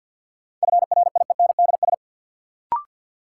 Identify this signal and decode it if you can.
Morse code, heard as a keyed tone, about 53 wpm, 700 Hz. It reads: FRIENDS
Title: Sentences from Top 1000 Words - Review Part 5 - New Word Drill - 50wpm